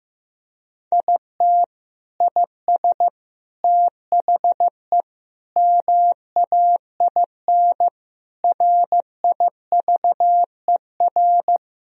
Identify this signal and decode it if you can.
Morse code, heard as a keyed tone, about 15 wpm, 700 Hz. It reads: IT IS THE MAIN RIVER
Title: Sentences from Top 1000 Words - Review Part 2 - 15wpm